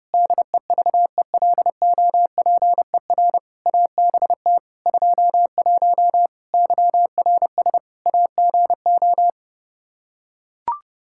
Morse code, heard as a keyed tone, about 30 wpm, 700 Hz. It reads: DEVELOPER ABT 21 YRS AGO
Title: Practice QSO - Exchange 3 - More Info - 30wpm